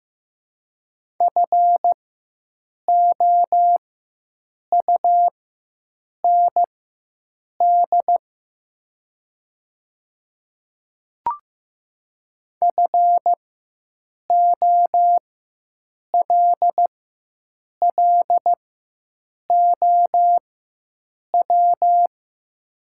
Morse code, heard as a keyed tone, about 15 wpm, spaced as for 7 wpm, 700 Hz. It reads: FOUND FOLLOW
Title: 4x Character Spacing - Top 500 Words - 15wpm